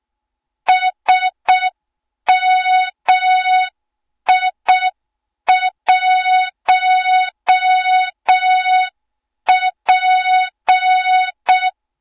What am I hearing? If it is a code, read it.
Morse code, heard as a keyed tone, about 6 wpm, 3700 Hz, with noise. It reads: SMI1P
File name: Morze-HA2MN / hang / 20 GYAKORLAT